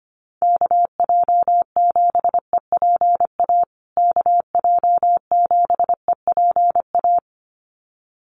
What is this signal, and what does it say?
Morse code, heard as a keyed tone, about 25 wpm, 700 Hz. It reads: KJ7EPA KJ7EPA
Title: USA Call Signs Repeated 2x 25wpm